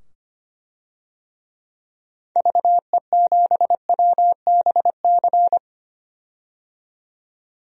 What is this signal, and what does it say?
Morse code, heard as a keyed tone, about 25 wpm, 700 Hz. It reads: VE7WBC
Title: DXCC-100 Call Signs - 6 Characters - 25wpm